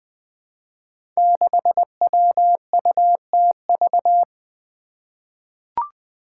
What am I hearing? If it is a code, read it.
Morse code, heard as a keyed tone, about 20 wpm, 700 Hz. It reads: 6WUTV
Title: Five Challenging Letter-Number Combinations 20wpm